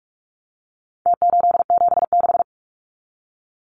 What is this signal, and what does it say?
Morse code, heard as a keyed tone, about 44 wpm, 700 Hz. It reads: T876
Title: Numbers - Contest Style - 4-Digit 45wpm